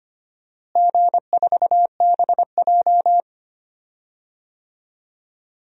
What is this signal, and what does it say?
Morse code, heard as a keyed tone, about 25 wpm, 700 Hz. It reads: G4BJ
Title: DXCC-100 Call Signs - 4 Characters - Repeated 2x - 25wpm